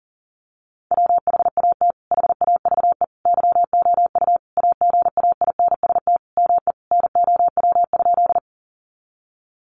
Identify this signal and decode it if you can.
Morse code, heard as a keyed tone, about 40 wpm, 700 Hz. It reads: WHAT HAVE YOU AGAINST ME NOW?
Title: Sentences from Top 400 Words 40wpm